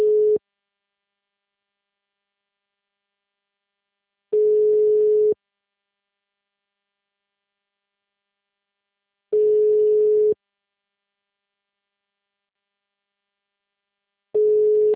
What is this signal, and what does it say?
Ringback tone (Europe, 425 Hz).